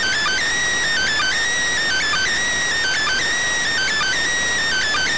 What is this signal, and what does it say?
Emergency Alert System (SAME) tones, heard as harsh buzzing bursts. It is heard from 0.9 to 5.0 s.